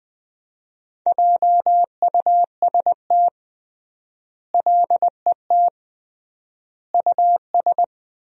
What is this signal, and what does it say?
Morse code, heard as a keyed tone, about 20 wpm, 700 Hz. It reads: JUST LET US
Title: 3x Word-Spacing - Sentences from Top 500 Words - 20wpm